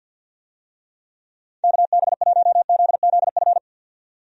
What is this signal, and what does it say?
Morse code, heard as a keyed tone, about 53 wpm, 700 Hz. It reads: KB1ZZF
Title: USA Call Signs - 6 Characters - 50wpm